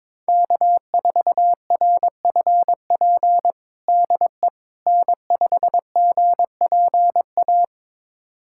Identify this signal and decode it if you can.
Morse code, heard as a keyed tone, about 22 wpm, 700 Hz. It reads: K4RFP DE N5GPA
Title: USA Callsign DE Callsign - 5 Characters - 22wpm